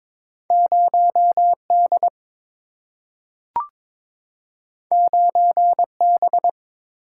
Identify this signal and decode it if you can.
Morse code, heard as a keyed tone, about 22 wpm, 700 Hz. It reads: 0D 9B
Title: Single-Letter Single-Number 22wpm